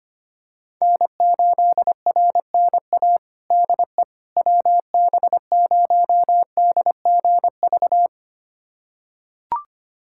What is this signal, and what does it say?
Morse code, heard as a keyed tone, about 25 wpm, 700 Hz. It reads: N8RNA DE WB0DGV
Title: USA Callsign DE Callsign 25wpm